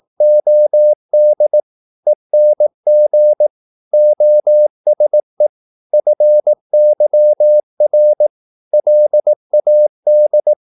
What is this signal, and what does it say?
Morse code, heard as a keyed tone, about 18 wpm, 600 Hz. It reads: OD ENG OSE FYR LAD
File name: Korte ord DK 18wpm